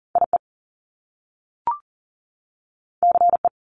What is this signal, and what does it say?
Morse code, heard as a keyed tone, about 40 wpm, 700 Hz. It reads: IE CE
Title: Top Two Letter Combinations in Words 40wpm